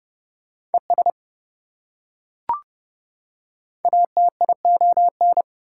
Morse code, heard as a keyed tone, about 30 wpm, 700 Hz. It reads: ES ATION
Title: Prefixes and Suffixes 30wpm